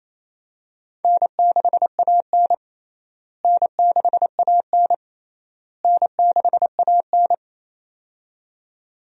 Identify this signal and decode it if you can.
Morse code, heard as a keyed tone, about 28 wpm, 700 Hz. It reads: N6AN N6AN N6AN
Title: USA Call Signs - 4 Characters - Repeated 3x - 28wpm